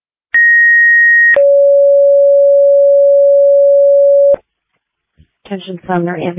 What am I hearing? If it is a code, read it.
Two-tone page, heard as a one-second tone, then a three-second tone, about 1900 then 560 Hz, with speech. It is heard from 0.3 to 4.3 s.